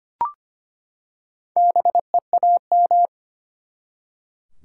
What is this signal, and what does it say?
Morse code, heard as a keyed tone, about 25 wpm, 700 Hz. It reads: BEAM
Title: Top 90 QSO Elements 25wpm